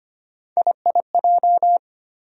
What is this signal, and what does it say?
Morse code, heard as a keyed tone, about 25 wpm, 700 Hz. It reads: IIJ
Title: USA Call Sign Postfixes - 3 Letters 25wpm